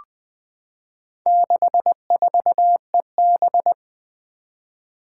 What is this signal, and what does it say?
Morse code, heard as a keyed tone, about 20 wpm, 700 Hz. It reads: 64EB
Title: Four Challenging Letter-Number Combinations 20wpm